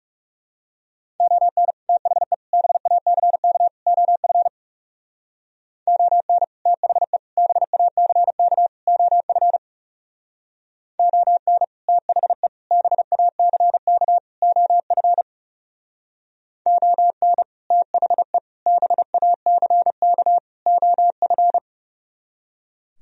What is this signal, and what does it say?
Morse code, heard as a keyed tone, about 34 wpm, 700 Hz. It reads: ON THE BACK OF ON THE BACK OF ON THE BACK OF ON THE BACK OF
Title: Speed-Racing - Sets of 4 Words 30wpm